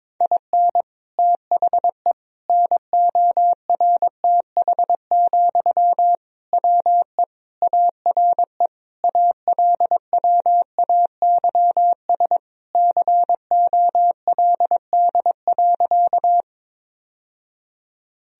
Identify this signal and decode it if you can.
Morse code, heard as a keyed tone, about 22 wpm, 700 Hz. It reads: IN THE NORTH, WE ARE ALWAYS COLD.
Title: Sentences from Top 1000 Words - Review Part 1 - 22wpm